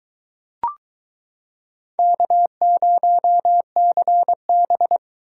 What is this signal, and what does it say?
Morse code, heard as a keyed tone, about 22 wpm, 700 Hz. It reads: K0CB